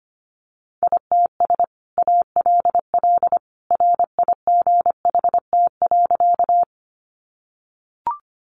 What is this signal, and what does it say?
Morse code, heard as a keyed tone, about 25 wpm, 700 Hz. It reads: ITS ALL RIGHT.
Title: Sentences from Top 600 Words 25wpm